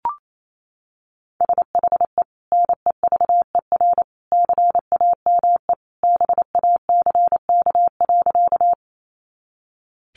Morse code, heard as a keyed tone, about 28 wpm, 700 Hz. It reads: SHE NEVER CAME BACK.
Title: Sentences from Top 300 Words 28wpm